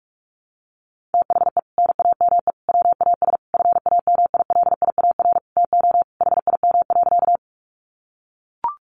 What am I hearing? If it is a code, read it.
Morse code, heard as a keyed tone, about 44 wpm, 700 Hz. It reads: THE NAME WAS FAMILIAR TO HIM.
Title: Sentences from Top 1000 Words 45wpm